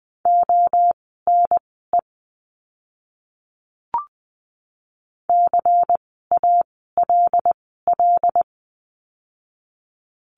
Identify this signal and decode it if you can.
Morse code, heard as a keyed tone, about 19 wpm, spaced as for 14 wpm, 700 Hz. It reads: ONE CALL
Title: 230 Lesson 37f - Letter x - Words 20wpm